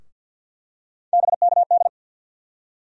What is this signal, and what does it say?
Morse code, heard as a keyed tone, about 52 wpm, 700 Hz. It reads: BKD
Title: USA Call Sign Postfixes - 3 Letters 50wpm